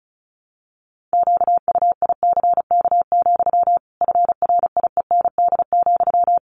Morse code, heard as a keyed tone, about 34 wpm, 700 Hz. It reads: QUICK, FRIEND,
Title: Sentences from Top 500 Words 35wpm